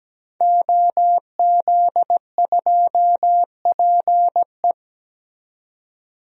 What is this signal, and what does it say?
Morse code, heard as a keyed tone, about 17 wpm, 700 Hz. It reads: OZ2PE
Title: DXCC-100 Call Signs - 5 Characters - 17wpm